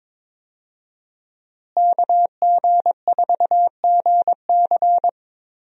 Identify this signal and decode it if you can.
Morse code, heard as a keyed tone, about 22 wpm, 700 Hz. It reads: KG4GC
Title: USA Call Signs - 5 Characters - Repeated 3x - 22wpm